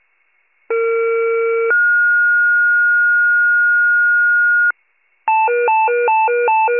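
Two-tone page, heard as a one-second tone, then a three-second tone, about 460 then 1430 Hz. It is heard from 0.7 to 4.7 s.
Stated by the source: County-wide fire paging tones.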